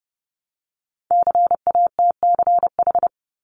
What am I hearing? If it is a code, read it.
Morse code, heard as a keyed tone, about 30 wpm, 700 Hz. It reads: CATCH
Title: Top 1251-1500 Words 30wpm